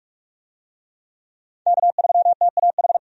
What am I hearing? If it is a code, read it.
Morse code, heard as a keyed tone, about 46 wpm, 700 Hz. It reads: K3TAH
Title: USA Call Signs - 5 Characters - 45wpm